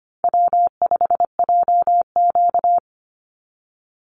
Morse code, heard as a keyed tone, about 25 wpm, 700 Hz. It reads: W5JQ
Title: Easy Call Signs Repeated 2x - 25wpm